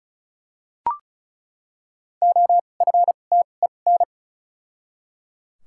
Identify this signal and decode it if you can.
Morse code, heard as a keyed tone, about 36 wpm, spaced as for 25 wpm, 700 Hz. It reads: OFTEN
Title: 2x Character Spacing - Top 300 Words - 35wpm